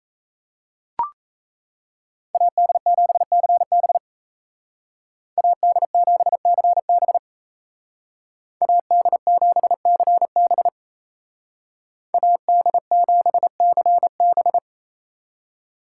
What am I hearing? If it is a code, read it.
Morse code, heard as a keyed tone, about 37 wpm, 700 Hz. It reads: AD7CB AD7CB AD7CB AD7CB